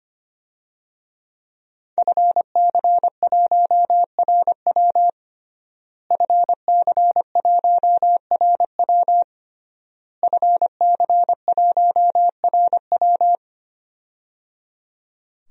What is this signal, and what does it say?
Morse code, heard as a keyed tone, about 25 wpm, 700 Hz. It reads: FC1RW FC1RW FC1RW
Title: DXCC-100 Call Signs - 5 Characters - Repeated 3x - 25wpm